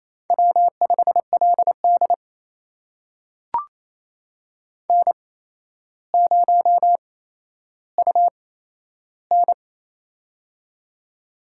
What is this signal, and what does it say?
Morse code, heard as a keyed tone, about 28 wpm, spaced as for 12 wpm, 700 Hz. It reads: W5LD N0UN
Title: USA Call Signs - 4 Characters - 8x Character Spacing - 28wpm